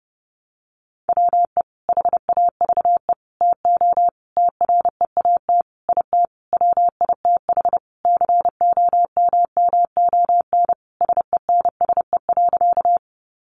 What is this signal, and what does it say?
Morse code, heard as a keyed tone, about 30 wpm, 700 Hz. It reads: WE HAVE TO TREAT IT WITH COMMON SENSE.
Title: Sentences from Top 1300 Words 30wpm